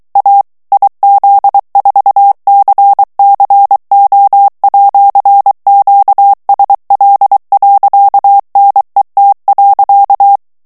Morse code, heard as a keyed tone, about 23 wpm, 800 Hz. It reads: A IZ4CCO@QSL.NET.
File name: morse mp3 / ITALIEN texte clair 1 vitesse 22 mots minute